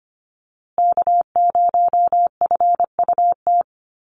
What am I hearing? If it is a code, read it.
Morse code, heard as a keyed tone, about 25 wpm, 700 Hz. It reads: K0FUT
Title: USA Call Signs Repeated 3x 25wpm